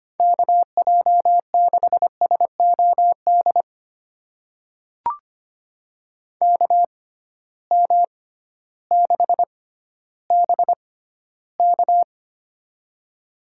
Morse code, heard as a keyed tone, about 25 wpm, spaced as for 14 wpm, 700 Hz. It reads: KJ6SOD KM6BK
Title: USA Call Signs - 6x Character Spacing - 25wpm